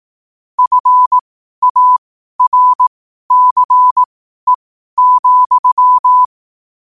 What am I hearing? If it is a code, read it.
Morse code, heard as a keyed tone, about 18 wpm, spaced as for 12 wpm, 1000 Hz. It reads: FARCE,
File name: qotdfiles / 12wpm